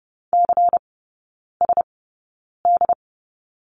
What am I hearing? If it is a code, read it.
Morse code, heard as a keyed tone, about 30 wpm, spaced as for 9 wpm, 700 Hz. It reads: CSD
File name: Lesson 19e - Letter w - Three Characters 30wpm